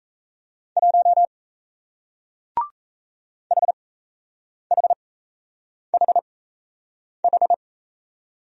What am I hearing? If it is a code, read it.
Morse code, heard as a keyed tone, about 38 wpm, spaced as for 8 wpm, 700 Hz. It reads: 1 HHHH